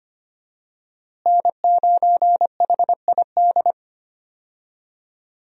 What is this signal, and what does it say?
Morse code, heard as a keyed tone, about 25 wpm, 700 Hz. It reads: N9HID
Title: USA Call Signs - 5 Characters - Repeated 2x - 25wpm